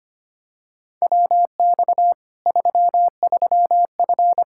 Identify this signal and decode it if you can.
Morse code, heard as a keyed tone, about 25 wpm, 700 Hz. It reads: WX 33F
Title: Practice QSO - Exchange 2 - Info - 25wpm